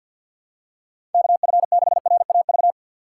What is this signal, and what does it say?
Morse code, heard as a keyed tone, about 52 wpm, 700 Hz. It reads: KF6RAV
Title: USA Call Signs - 6 Characters - 3x Character Spacing - 50wpm